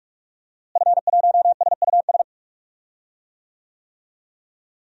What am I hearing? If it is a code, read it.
Morse code, heard as a keyed tone, about 43 wpm, 700 Hz. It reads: F1SUS
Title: DXCC-100 Call Signs - 5 Characters - Repeated 3x - 45wpm